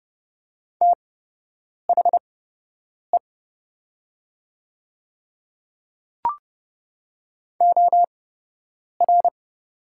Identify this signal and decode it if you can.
Morse code, heard as a keyed tone, about 30 wpm, spaced as for 8 wpm, 700 Hz. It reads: THE OR
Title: 8x Character Spacing - Top 100 Words - 30wpm